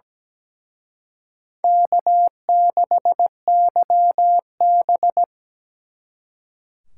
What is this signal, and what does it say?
Morse code, heard as a keyed tone, about 20 wpm, 700 Hz. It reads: K6YB
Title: Speed Racing - USA 1x2 Call Signs 17wpm